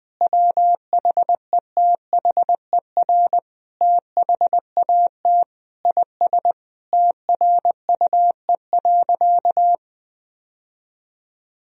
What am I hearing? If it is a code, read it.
Morse code, heard as a keyed tone, about 20 wpm, 700 Hz. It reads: WHETHER THAT IS TRUE.
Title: Sentences from Top 1000 Words - Review Part 4 - 20wpm